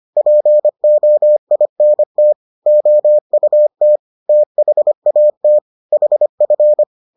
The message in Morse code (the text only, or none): POINT OUT THAT HF